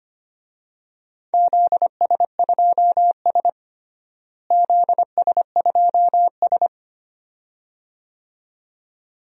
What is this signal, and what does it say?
Morse code, heard as a keyed tone, about 25 wpm, 700 Hz. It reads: ZS2S ZS2S
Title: DXCC-100 Call Signs - 4 Characters - Repeated 2x - 25wpm